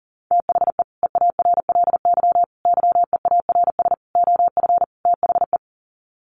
Morse code, heard as a keyed tone, about 40 wpm, 700 Hz. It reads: THE EARLY YEARS OF THE
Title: Sets of 5 Words Encore 40wpm